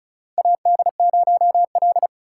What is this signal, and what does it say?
Morse code, heard as a keyed tone, about 34 wpm, 700 Hz. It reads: AD0L
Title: USA Call Signs - 4 Characters - 4x Character Spacing - 35wpm